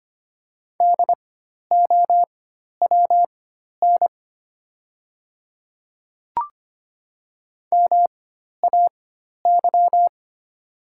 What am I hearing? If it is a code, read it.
Morse code, heard as a keyed tone, about 25 wpm, spaced as for 12 wpm, 690 Hz. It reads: DOWN MAY
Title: 4x Character Spacing - Top 200 Words - 25wpm